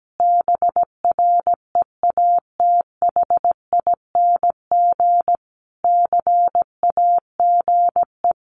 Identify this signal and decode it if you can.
Morse code, heard as a keyed tone, about 17 wpm, 700 Hz. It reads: BREATHING CAGE
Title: A Sound Of Thunder 17wpm